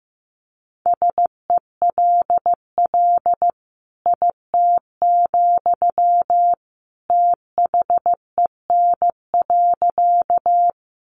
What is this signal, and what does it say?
Morse code, heard as a keyed tone, about 14 wpm, 700 Hz. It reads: SELL IT, THEN.
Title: Sentences from Top 1100 Words 15wpm